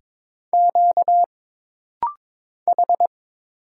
Morse code, heard as a keyed tone, about 22 wpm, 700 Hz. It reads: Q H
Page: This rapid-fire timing is geared to push your ICR (Instant Character Recognition) skills.